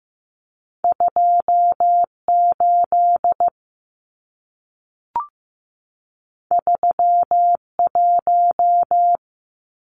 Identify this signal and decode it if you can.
Morse code, heard as a keyed tone, about 15 wpm, 700 Hz. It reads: 28 31
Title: Numbers - 2-Digit 15wpm